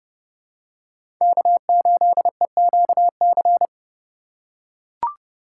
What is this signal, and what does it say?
Morse code, heard as a keyed tone, about 30 wpm, 700 Hz. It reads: K8EQC
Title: USA Call Signs - 5 Characters - 4x Character Spacing - 30wpm